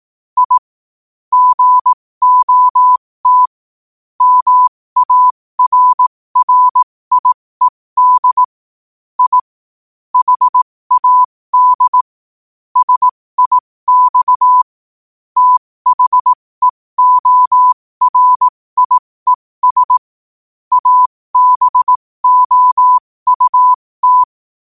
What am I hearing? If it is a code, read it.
Morse code, heard as a keyed tone, about 18 wpm, spaced as for 15 wpm, 1000 Hz. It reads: I GOT MARRIED I HAD SIX THEORIES ABOUT
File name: qotdfiles / 15wpm